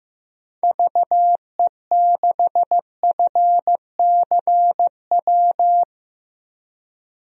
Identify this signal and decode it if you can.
Morse code, heard as a keyed tone, about 15 wpm, 700 Hz. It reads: VE6FCW
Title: DXCC-100 Call Signs - 6 Characters - 15wpm